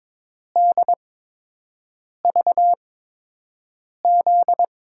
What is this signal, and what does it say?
Morse code, heard as a keyed tone, about 21 wpm, spaced as for 6 wpm, 700 Hz. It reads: DVZ
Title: USA Call Signs - 6 Characters - 8x Character Spacing - 22wpm